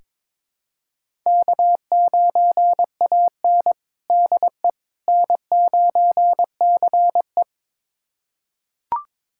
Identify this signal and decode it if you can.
Morse code, heard as a keyed tone, about 22 wpm, 700 Hz. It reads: K9AN DE N9CE
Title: USA Callsign DE Callsign - 4 Characters - 22wpm